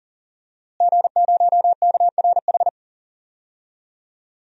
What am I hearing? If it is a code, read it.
Morse code, heard as a keyed tone, about 40 wpm, 700 Hz. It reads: G0KRH
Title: DXCC-100 Call Signs - 5 Characters - Repeated 2x - 40wpm